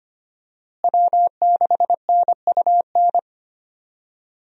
Morse code, heard as a keyed tone, about 25 wpm, 700 Hz. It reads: W6NUN
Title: USA Call Signs - 5 Characters - Repeated 2x - 25wpm